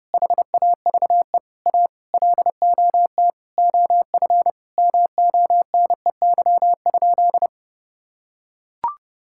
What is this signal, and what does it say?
Morse code, heard as a keyed tone, about 32 wpm, 700 Hz. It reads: HAVE A LOT OF MONEY?
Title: Sentences from Top 1000 Words - Review Part 5 - 30wpm